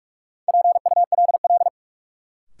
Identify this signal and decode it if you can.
Morse code, heard as a keyed tone, about 46 wpm, 700 Hz. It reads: PULL
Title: Sentences from Top 400 Words - New Word Drill - 45wpm